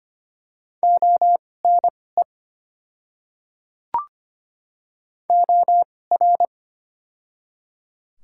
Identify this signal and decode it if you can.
Morse code, heard as a keyed tone, about 25 wpm, spaced as for 18 wpm, 700 Hz. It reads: ONE OR
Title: 2x Character Spacing - Top 200 Words - 25wpm